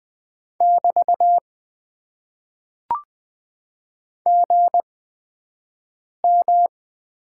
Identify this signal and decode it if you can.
Morse code, heard as a keyed tone, about 20 wpm, spaced as for 5 wpm, 700 Hz. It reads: <BT> GM
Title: Top 90 QSO Elements - 8x Character Spacing - 20wpm